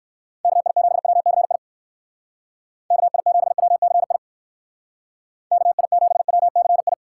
Morse code, heard as a keyed tone, about 57 wpm, 700 Hz. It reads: KI7WCI KI7WCI KI7WCI